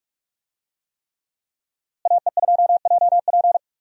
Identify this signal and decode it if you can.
Morse code, heard as a keyed tone, about 41 wpm, 700 Hz. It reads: AE2JP
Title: USA Call Signs - 5 Characters - Repeated 3x - 45wpm